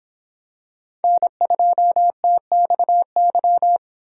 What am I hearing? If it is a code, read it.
Morse code, heard as a keyed tone, about 25 wpm, 700 Hz. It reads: N2TXY